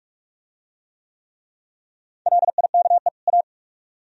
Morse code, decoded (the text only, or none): LIKE A